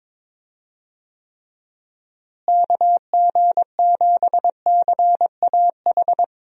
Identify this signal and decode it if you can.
Morse code, heard as a keyed tone, about 22 wpm, 700 Hz. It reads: KG7CAH
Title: USA Call Signs - 6 Characters - Repeated 2x - 22wpm